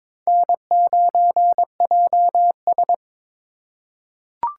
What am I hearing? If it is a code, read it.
Morse code, heard as a keyed tone, about 22 wpm, 700 Hz. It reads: N9JS
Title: USA Call Signs - 4 Characters - 6x Character Spacing - 22wpm